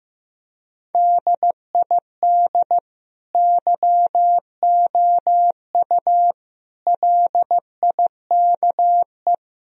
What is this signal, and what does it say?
Morse code, heard as a keyed tone, about 15 wpm, 700 Hz. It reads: DID YOU LIKE
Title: Sentences from Top 1000 Words - Review Part 4 - 15wpm